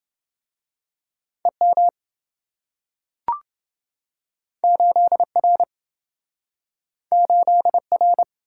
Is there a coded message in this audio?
Yes: Morse code from 1.4 to 8.2 s, heard as a keyed tone, about 26 wpm, 700 Hz. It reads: EM 8R 8R